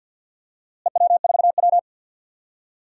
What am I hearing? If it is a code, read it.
Morse code, heard as a keyed tone, about 47 wpm, 700 Hz. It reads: EW4W
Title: DXCC-100 Call Signs - 4 Characters - Repeated 3x - 50wpm